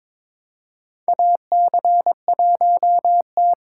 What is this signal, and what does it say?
Morse code, heard as a keyed tone, about 22 wpm, 700 Hz. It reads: AC1T
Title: USA Call Signs - 4 Characters - 4x Character Spacing - 22wpm